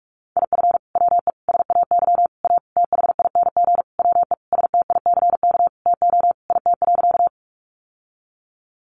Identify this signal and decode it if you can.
Morse code, heard as a keyed tone, about 44 wpm, 700 Hz. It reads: IF WE SAY A THING WE STICK TO IT.
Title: Sentences from Top 1200 Words 45wpm